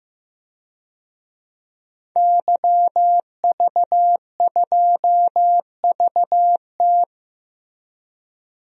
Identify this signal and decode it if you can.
Morse code, heard as a keyed tone, about 15 wpm, 700 Hz. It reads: YV2VT
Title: DXCC-100 Call Signs - 5 Characters - Repeated 3x - 15wpm